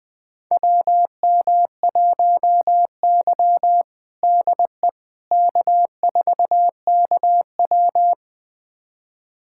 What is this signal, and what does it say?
Morse code, heard as a keyed tone, about 20 wpm, 700 Hz. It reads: WM1Y DE K4KW
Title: USA Callsign DE Callsign - 4 Characters - 20wpm